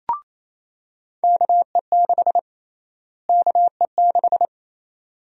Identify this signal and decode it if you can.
Morse code, heard as a keyed tone, about 27 wpm, 700 Hz. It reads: KE6 KE6
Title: USA Call Sign Prefixes - Repeated 2x - 28wpm